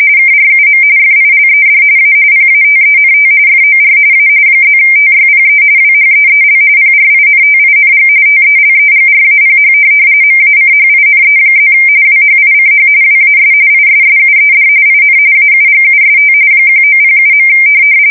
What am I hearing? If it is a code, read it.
RTTY, about 45.45 baud. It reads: KRUM FILED A PATENT FOR A +TYPE WHEEL PRINTING TELEGRAPH MACHINE+ WHICH WAS ISSUED IN AUGUST 1907.
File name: fsk170 / teletype-corp